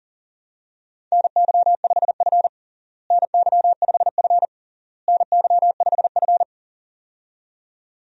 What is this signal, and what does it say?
Morse code, heard as a keyed tone, about 37 wpm, 700 Hz. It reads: NY5F NY5F NY5F
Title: USA Call Signs - 4 Characters - Repeated 3x - 40wpm